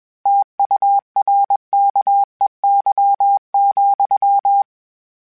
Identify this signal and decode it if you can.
Morse code, heard as a keyed tone, about 21 wpm, 800 Hz. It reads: TURKEY,
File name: Misc / Misc-mp3s / Word Pairs @20